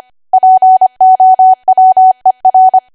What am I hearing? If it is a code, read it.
Morse code, heard as a keyed tone, about 25 wpm, 750 Hz. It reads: POWER
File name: cwa / W1AW-Level3 / 25WPM